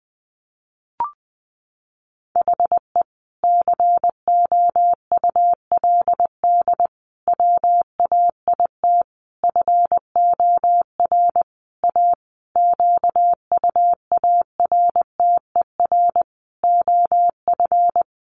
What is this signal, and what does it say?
Morse code, heard as a keyed tone, about 20 wpm, 700 Hz. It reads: HE COULD WAIT FOR A QUARTER OF
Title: Sentences from Top 1300 Words 20wpm